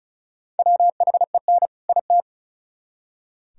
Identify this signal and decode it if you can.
Morse code, heard as a keyed tone, about 44 wpm, 700 Hz. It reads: WHEN IT
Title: Speed-Racing - Sets of 2 Words 35wpm